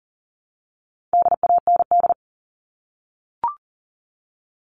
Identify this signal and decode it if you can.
Morse code, heard as a keyed tone, about 40 wpm, 700 Hz. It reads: DAND
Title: All Cut Numbers - 4-Digit 40wpm